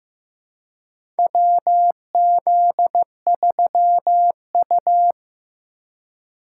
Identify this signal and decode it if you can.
Morse code, heard as a keyed tone, about 15 wpm, 700 Hz. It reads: WZ3U
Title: USA Call Signs - 4 Characters - 2x Character Spacing - 15wpm